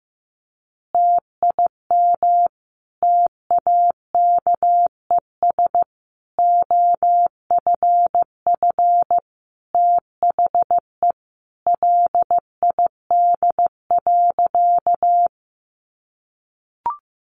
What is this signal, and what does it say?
Morse code, heard as a keyed tone, about 15 wpm, 700 Hz. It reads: TIM TAKES OFF THE LID.
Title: Easy Phrases 15wpm